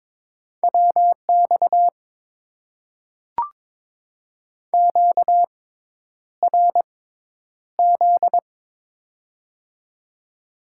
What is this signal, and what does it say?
Morse code, heard as a keyed tone, about 22 wpm, spaced as for 10 wpm, 700 Hz. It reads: WX QRZ
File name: Top 90 QSO Elements - 6x Character Spacing - 22wpm